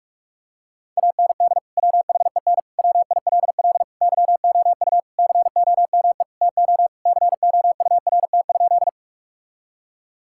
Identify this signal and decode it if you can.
Morse code, heard as a keyed tone, about 46 wpm, 700 Hz. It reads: AND WHEN WILL YOU COME TO COURT?
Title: Sentences from Top 1600 Words 45wpm